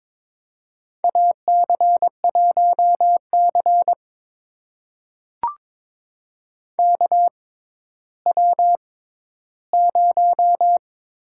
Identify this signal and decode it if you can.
Morse code, heard as a keyed tone, about 22 wpm, spaced as for 13 wpm, 700 Hz. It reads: AC1C KW0
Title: USA Call Signs - 4 Characters - 6x Character Spacing - 22wpm